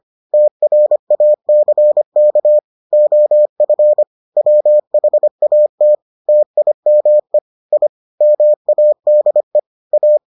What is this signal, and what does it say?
Morse code, heard as a keyed tone, about 25 wpm, 600 Hz. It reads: TRACK OF WHAT TIME I MADE A